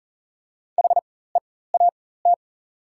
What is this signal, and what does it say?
Morse code, heard as a keyed tone, about 40 wpm, spaced as for 19 wpm, 690 Hz. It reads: HEAT
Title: 4x Character Spacing - Top 500 Words - 40wpm